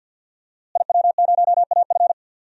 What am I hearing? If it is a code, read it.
Morse code, heard as a keyed tone, about 49 wpm, 700 Hz. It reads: IW0AF
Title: DXCC-100 Call Signs - 5 Characters - Repeated 2x - 50wpm